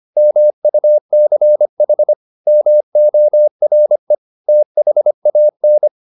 Morse code, heard as a keyed tone, about 25 wpm, 600 Hz. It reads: MUCH MORE THAN